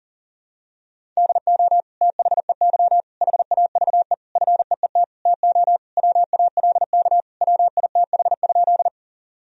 Morse code, heard as a keyed tone, about 40 wpm, 700 Hz. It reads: DO THEY HAVE FEET TO WALK WITH?
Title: Sentences from Top 1000 Words - Review Part 1 - 40wpm